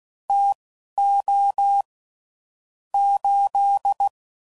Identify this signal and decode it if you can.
Morse code, heard as a keyed tone, about 16 wpm, spaced as for 11 wpm, 800 Hz. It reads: TO 8
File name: CW Code Groups @10WPM / QSOs / QSO WA7QOX